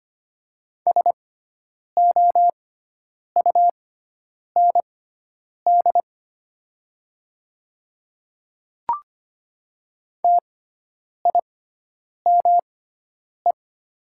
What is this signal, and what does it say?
Morse code, heard as a keyed tone, about 25 wpm, spaced as for 9 wpm, 700 Hz. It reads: SOUND TIME
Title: 6x Character Spacing - Top 100 Words - 25wpm